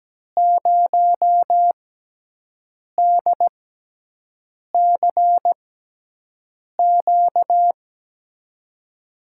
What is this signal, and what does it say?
Morse code, heard as a keyed tone, about 17 wpm, spaced as for 6 wpm, 700 Hz. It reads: 0DCQ
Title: USA Call Signs - 5 Characters - 6x Character Spacing - 17wpm